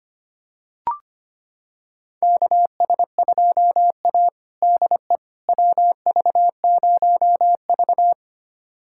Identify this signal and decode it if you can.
Morse code, heard as a keyed tone, about 25 wpm, 700 Hz. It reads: KS2A DE WV0V
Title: USA Callsign DE Callsign - 4 Characters - 25wpm